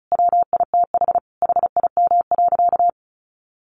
Morse code, heard as a keyed tone, about 34 wpm, 700 Hz. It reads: WITH HIM.
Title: Sentences from Top 800 Words 35wpm